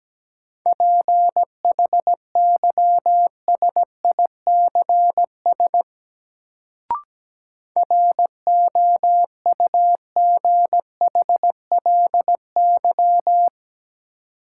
Morse code, heard as a keyed tone, about 17 wpm, 700 Hz. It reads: PHYSICS ROUGHLY
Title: Sentences from Top 1500 Words - New Word Drill Alt - 17wpm